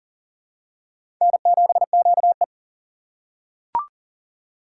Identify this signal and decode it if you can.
Morse code, heard as a keyed tone, about 40 wpm, 700 Hz. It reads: N7QE
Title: USA Call Signs - 4 Characters - 6x Character Spacing - 40wpm